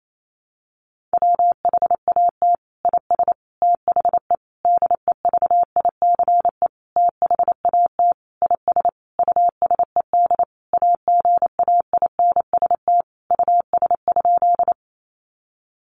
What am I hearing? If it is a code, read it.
Morse code, heard as a keyed tone, about 28 wpm, 700 Hz. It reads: WHAT IS THE DEVICE THAT IS USED AGAINST US?
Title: Sentences from Top 1500 Words 28wpm